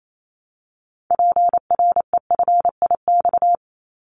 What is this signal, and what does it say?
Morse code, heard as a keyed tone, about 28 wpm, 700 Hz. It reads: PREFIX
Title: Callsign Prefixes for Top 100 DXCC - 28wpm